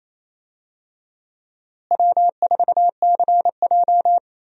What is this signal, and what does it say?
Morse code, heard as a keyed tone, about 28 wpm, 700 Hz. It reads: W4CJ
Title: USA Call Signs - 4 Characters - Repeated 2x - 28wpm